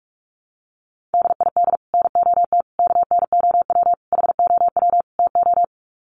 Morse code, heard as a keyed tone, about 44 wpm, 700 Hz. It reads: DID NOT KNOW HOW TO
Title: Sets of 5 Words 45wpm